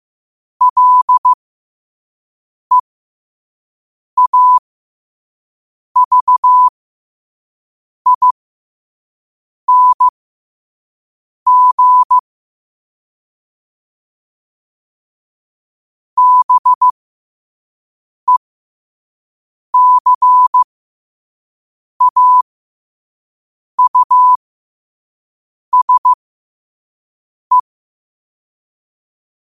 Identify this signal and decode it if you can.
Morse code, heard as a keyed tone, about 15 wpm, spaced as for 5 wpm, 1000 Hz. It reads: LEAVING BECAUSE
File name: qotdfiles / 5wpm